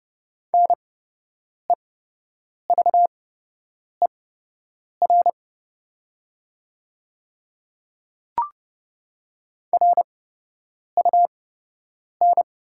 Morse code, heard as a keyed tone, about 30 wpm, spaced as for 8 wpm, 700 Hz. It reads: NEVER RUN
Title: 8x Character Spacing - Top 500 Words - 30wpm